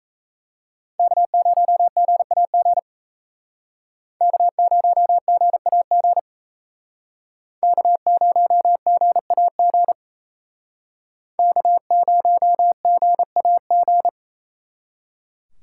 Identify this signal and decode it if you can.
Morse code, heard as a keyed tone, about 35 wpm, 700 Hz. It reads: K0GAG K0GAG K0GAG K0GAG